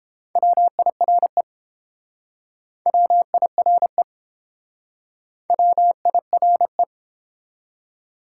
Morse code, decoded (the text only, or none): WIRE WIRE WIRE